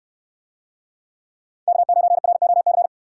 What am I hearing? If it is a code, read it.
Morse code, heard as a keyed tone, about 69 wpm, 700 Hz. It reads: K9ROC